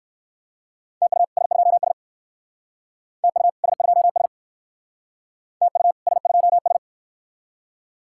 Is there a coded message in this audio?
Yes: Morse code from 1.0 to 6.8 s, heard as a keyed tone, about 69 wpm, 700 Hz. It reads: TU S2S TU S2S TU S2S